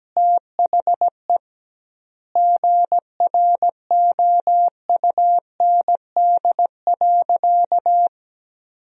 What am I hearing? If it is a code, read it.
Morse code, heard as a keyed tone, about 17 wpm, 700 Hz. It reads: THE GROUND.
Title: 2x Word-Spacing - Sentences from Top 500 Words - 17wpm